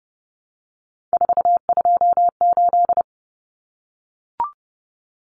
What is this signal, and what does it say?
Morse code, heard as a keyed tone, about 30 wpm, 700 Hz. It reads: 428
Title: Common Cut Numbers - 3-Digit 30wpm